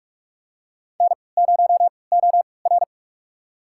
Morse code, heard as a keyed tone, about 46 wpm, spaced as for 26 wpm, 700 Hz. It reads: N0OR
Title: USA Call Signs - 3x Character Spacing - 45wpm